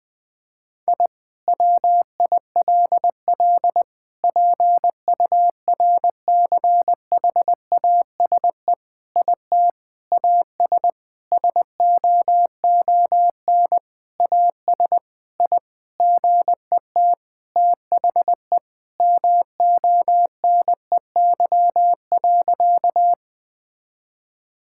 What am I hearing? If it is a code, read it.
Morse code, heard as a keyed tone, about 21 wpm, 700 Hz. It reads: I WILL PURCHASE IT AS SOON AS I GET THE MONEY.
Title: Sentences from Top 1000 Words - Review Part 5 - 20wpm